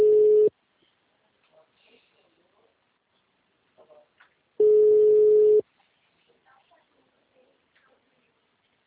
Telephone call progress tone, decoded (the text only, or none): ringback tone